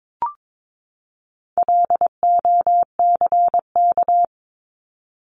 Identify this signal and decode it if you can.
Morse code, heard as a keyed tone, about 22 wpm, 700 Hz. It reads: LOCK
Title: Top 1001-1250 Words 22wpm